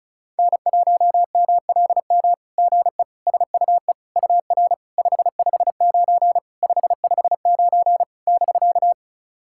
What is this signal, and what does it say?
Morse code, heard as a keyed tone, about 36 wpm, 700 Hz. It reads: N1MLM GE SUE UR 559 559 <BK>
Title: SOTA QSO Practice 35wpm